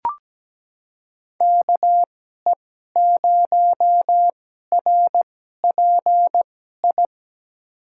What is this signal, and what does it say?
Morse code, heard as a keyed tone, about 17 wpm, spaced as for 12 wpm, 700 Hz. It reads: KE0RPI
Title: USA Call Signs - 6 Characters - 2x Character Spacing - 17wpm